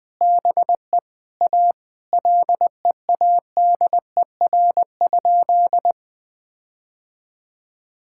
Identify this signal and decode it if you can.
Morse code, heard as a keyed tone, about 20 wpm, 700 Hz. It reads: BE A LEADER?
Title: Sentences with 7-Word Phrases - Spoken Beginning - 20wpm